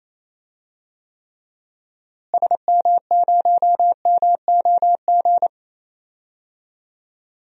Morse code, heard as a keyed tone, about 28 wpm, 700 Hz. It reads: SM0MOG
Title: DXCC-100 Call Signs - 6 Characters - Repeated 2x - 28wpm